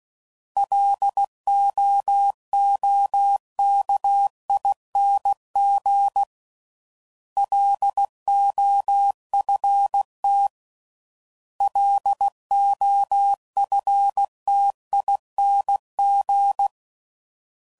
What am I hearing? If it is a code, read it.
Morse code, heard as a keyed tone, about 16 wpm, 800 Hz. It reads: LOOKING LOFT LOFTING